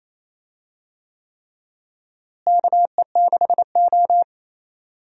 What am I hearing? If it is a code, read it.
Morse code, heard as a keyed tone, about 28 wpm, 700 Hz. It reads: KE6O
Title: USA Call Signs - 4 Characters - Repeated 3x - 28wpm